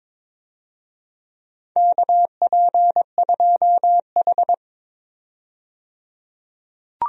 Morse code, heard as a keyed tone, about 22 wpm, 700 Hz. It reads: KP2H
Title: DXCC-100 Call Signs - 4 Characters - 22wpm